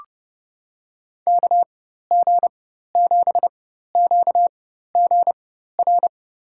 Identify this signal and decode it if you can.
Morse code, heard as a keyed tone, about 30 wpm, spaced as for 14 wpm, 700 Hz. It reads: KG7QGR
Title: USA Call Signs - 6 Characters - 4x Character Spacing - 30wpm